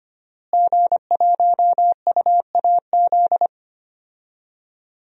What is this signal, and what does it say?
Morse code, heard as a keyed tone, about 25 wpm, 700 Hz. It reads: G1UAZ
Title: DXCC-100 Call Signs - 5 Characters - Repeated 2x - 25wpm